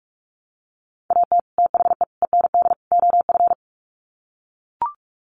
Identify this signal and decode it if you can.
Morse code, heard as a keyed tone, about 44 wpm, 700 Hz. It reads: AT THE END OF
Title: Sets of 4 Words 45wpm